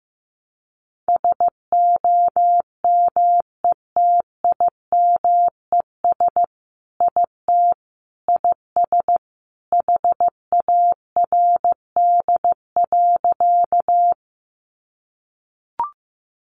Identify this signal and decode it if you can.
Morse code, heard as a keyed tone, about 15 wpm, 700 Hz. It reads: SOMETIMES IT IS HARD.
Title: Sentences from Top 600 Words 15wpm